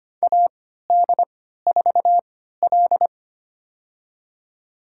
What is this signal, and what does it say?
Morse code, heard as a keyed tone, about 25 wpm, spaced as for 14 wpm, 700 Hz. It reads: AD4L
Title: USA Call Signs - 3x Character Spacing - 25wpm